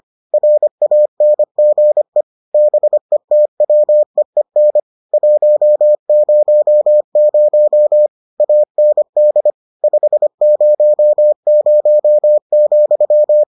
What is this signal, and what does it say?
Morse code, heard as a keyed tone, about 25 wpm, 600 Hz. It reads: RANGE BETWEEN 100 AND 500,